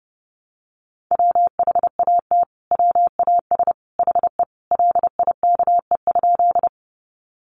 Morse code, heard as a keyed tone, about 30 wpm, 700 Hz. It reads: WHAT WAS HE LIKE?
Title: Sentences from Top 100 Words 30wpm